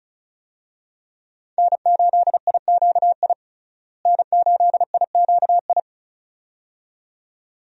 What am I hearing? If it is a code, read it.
Morse code, heard as a keyed tone, about 36 wpm, 700 Hz. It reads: N8IQI N8IQI
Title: USA Call Signs - 5 Characters - Repeated 2x - 35wpm